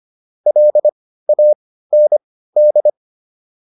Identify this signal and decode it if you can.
Morse code, heard as a keyed tone, about 25 wpm, spaced as for 15 wpm, 600 Hz. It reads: LAND